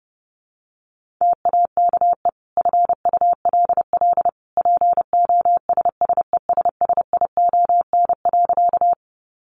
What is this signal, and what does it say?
Morse code, heard as a keyed tone, about 28 wpm, 700 Hz. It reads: TAKE FULL POSSESSION.
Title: Sentences from Top 1400 Words 30wpm